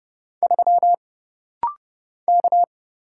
Morse code, heard as a keyed tone, about 31 wpm, 700 Hz. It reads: 3 K
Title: Single Letter-Number Alt - Rapid-Fire 30wpm